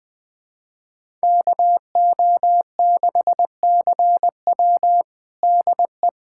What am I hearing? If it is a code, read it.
Morse code, heard as a keyed tone, about 20 wpm, 700 Hz. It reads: KO6CW DE
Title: USA Callsign DE Callsign - 5 Characters - 20wpm